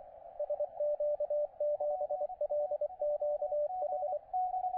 Morse code, heard as a keyed tone, about 24 wpm, 590 Hz, with radio noise. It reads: SQ7LQH
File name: OX3LX, NA-243 30CW